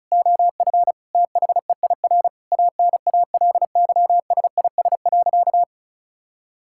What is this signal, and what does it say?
Morse code, heard as a keyed tone, about 36 wpm, 700 Hz. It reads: OF THEIR ANALYSIS.
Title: Sentences from Top 1000 Words - Review Part 5 - 35wpm